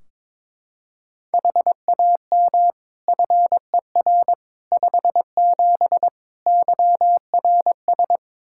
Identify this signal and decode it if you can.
Morse code, heard as a keyed tone, about 22 wpm, 700 Hz. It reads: HAM FER 57 YRS
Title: K1USN SST QSO Practice 22wpm